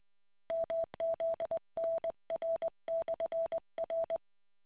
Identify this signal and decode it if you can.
Morse code, heard as a keyed tone, about 20 wpm, 660 Hz. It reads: MZNR/R